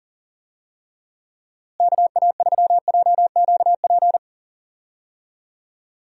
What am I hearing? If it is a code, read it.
Morse code, heard as a keyed tone, about 40 wpm, 700 Hz. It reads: KA3JQP
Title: USA Call Signs - 6 Characters - Repeated 3x - 40wpm